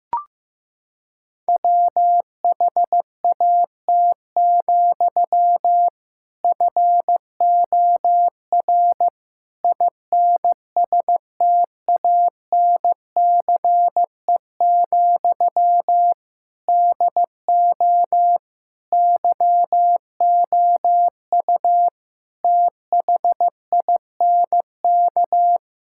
Morse code, heard as a keyed tone, about 15 wpm, 700 Hz. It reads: WHAT, FOR INSTANCE, DO YOU THINK
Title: Sentences from Top 1000 Words - Review Part 5 - 15wpm